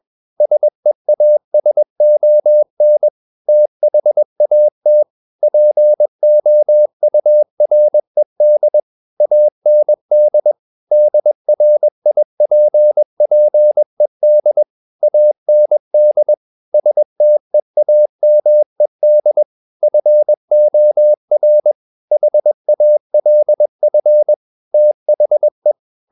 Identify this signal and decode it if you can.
Morse code, heard as a keyed tone, about 21 wpm, 600 Hz. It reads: SEASON THAT POURED AND DRIPPED AND STEAMED FOR HALF THE
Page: CW med Gnister